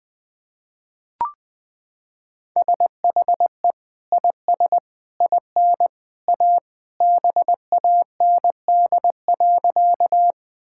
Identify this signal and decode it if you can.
Morse code, heard as a keyed tone, about 20 wpm, 700 Hz. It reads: SHE IS IN A BAND.
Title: Sentences from Top 1600 Words 20wpm